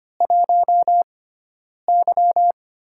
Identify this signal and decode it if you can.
Morse code, heard as a keyed tone, about 25 wpm, spaced as for 9 wpm, 700 Hz. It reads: 1Y
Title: USA Call Signs - 4 Characters - 6x Character Spacing - 25wpm